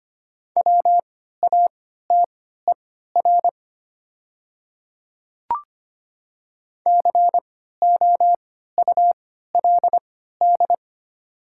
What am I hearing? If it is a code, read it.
Morse code, heard as a keyed tone, about 25 wpm, spaced as for 14 wpm, 700 Hz. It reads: WATER COULD
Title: 3x Character Spacing - Top 100 Words - 25wpm